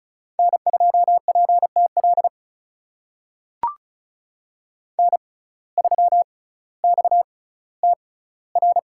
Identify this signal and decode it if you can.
Morse code, heard as a keyed tone, about 36 wpm, spaced as for 18 wpm, 700 Hz. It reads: N2PTL N3XTR
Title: USA Call Signs - 5 Characters - 6x Character Spacing - 35wpm